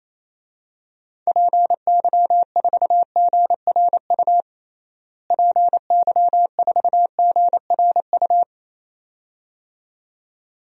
Morse code, heard as a keyed tone, about 28 wpm, 700 Hz. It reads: PY4GRU PY4GRU
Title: DXCC-100 Call Signs - 6 Characters - Repeated 2x - 28wpm